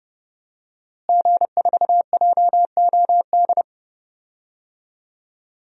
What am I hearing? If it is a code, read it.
Morse code, heard as a keyed tone, about 31 wpm, 700 Hz. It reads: G4JOD
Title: DXCC-100 Call Signs - 5 Characters - Repeated 3x - 30wpm